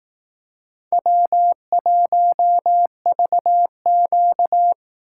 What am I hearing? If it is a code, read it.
Morse code, heard as a keyed tone, about 18 wpm, 700 Hz. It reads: W1VQ
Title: Speed Racing - USA Call Signs 15wpm